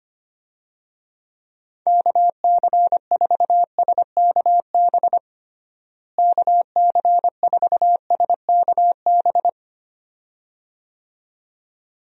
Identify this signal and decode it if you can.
Morse code, heard as a keyed tone, about 25 wpm, 700 Hz. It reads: KC4SKB KC4SKB
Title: USA Call Signs - 6 Characters - Repeated 2x - 25wpm